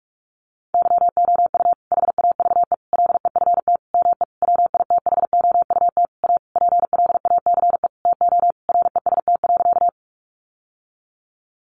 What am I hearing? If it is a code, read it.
Morse code, heard as a keyed tone, about 44 wpm, 700 Hz. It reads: YOU HAVE LEFT ME WITHOUT A PLACE TO REST.
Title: Sentences from Top 600 Words 45wpm